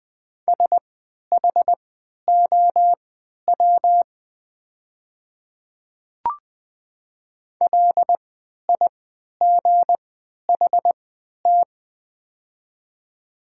Morse code, heard as a keyed tone, about 20 wpm, spaced as for 11 wpm, 700 Hz. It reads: SHOW LIGHT
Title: 3x Character Spacing - Top 300 Words - 20wpm